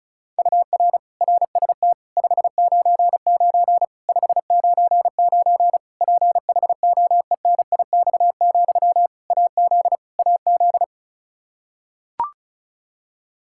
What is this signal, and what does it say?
Morse code, heard as a keyed tone, about 34 wpm, 700 Hz. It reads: UR RST 599 599 PHOENIX, AZ AZ
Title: K1USN SST QSO Practice 35wpm